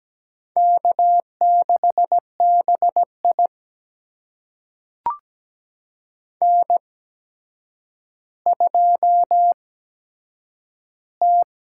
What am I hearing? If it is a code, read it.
Morse code, heard as a keyed tone, about 17 wpm, spaced as for 8 wpm, 700 Hz. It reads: K6BI N2T
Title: USA Call Signs - 4 Characters - 8x Character Spacing - 17wpm